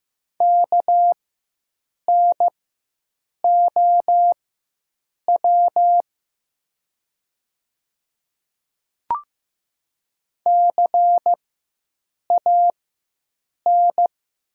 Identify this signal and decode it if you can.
Morse code, heard as a keyed tone, about 15 wpm, spaced as for 7 wpm, 700 Hz. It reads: KNOW CAN
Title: 4x Character Spacing - Top 100 Words - 15wpm